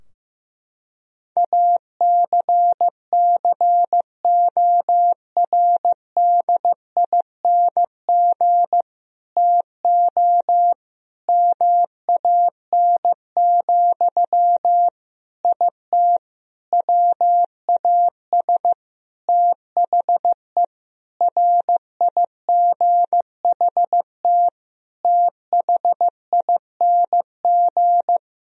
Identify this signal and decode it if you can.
Morse code, heard as a keyed tone, about 15 wpm, 700 Hz. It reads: ACCORDING TO MAN, IT WAS THE RIGHT THING
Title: Sentences from Top 1000 Words - Review Part 4 - 15wpm